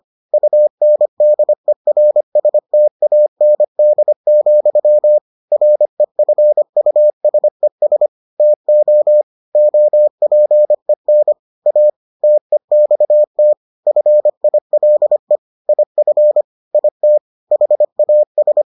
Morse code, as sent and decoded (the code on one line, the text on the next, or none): ..- -. -.. .-. ... - .- -. -.. --..-- / .-. ..-. ..- ... ... / - --- / --- .--. -. / .- / - . -..- - / ..-. .. .-.. / .. ..-. / .. - / .... .- ...
UNDERSTAND, REFUSES TO OPEN A TEXT FILE IF IT HAS